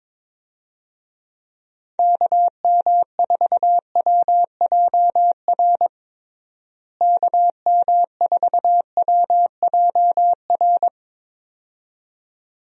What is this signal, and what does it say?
Morse code, heard as a keyed tone, about 22 wpm, 700 Hz. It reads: KM4WJR KM4WJR
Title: USA Call Signs - 6 Characters - Repeated 2x - 22wpm